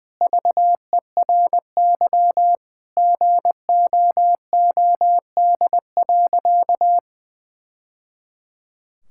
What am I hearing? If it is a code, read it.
Morse code, heard as a keyed tone, about 20 wpm, 700 Hz. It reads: VERY GOOD.
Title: Sentences from Top 1000 Words - Review Part 6 - 20wpm